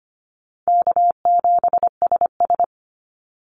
Morse code, heard as a keyed tone, about 25 wpm, 700 Hz. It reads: K7SS
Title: Easy Call Signs - 25wpm